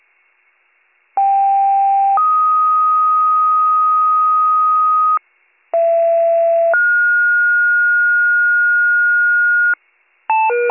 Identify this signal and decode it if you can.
Two-tone page, heard as a one-second tone, then a three-second tone, about 790 then 1250 Hz, about 670 then 1430 Hz, with noise. It is heard from 1.2 to 9.7 s.